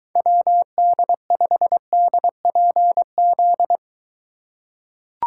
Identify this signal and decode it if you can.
Morse code, heard as a keyed tone, about 21 wpm, 700 Hz. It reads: WD5DPZ